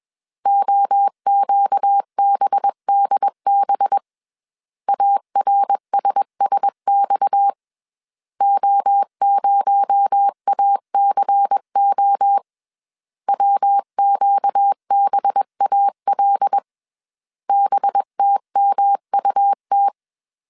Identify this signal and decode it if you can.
Morse code, heard as a keyed tone, about 21 wpm, 800 Hz. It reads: OQBDB ARSSX O0ACO WQBAL BTMUT
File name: CW / DET / 1 6 1500C S B 20wpm 2048